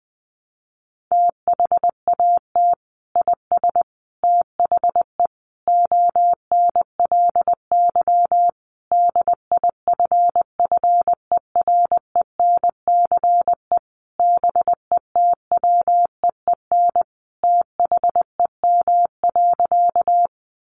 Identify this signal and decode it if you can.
Morse code, heard as a keyed tone, about 20 wpm, 700 Hz. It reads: THAT IS THE ONLY DIFFERENCE BETWEEN THEM.
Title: Sentences from Top 800 Words 20wpm